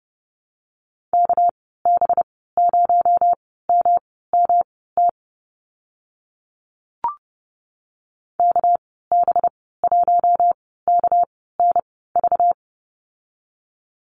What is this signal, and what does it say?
Morse code, heard as a keyed tone, about 30 wpm, spaced as for 17 wpm, 700 Hz. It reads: KB0MMT KB1KNV
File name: Lesson 37g - Letter x - Call Signs 30wpm